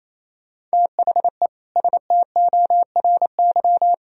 Morse code, heard as a keyed tone, about 37 wpm, 700 Hz. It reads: THE STORY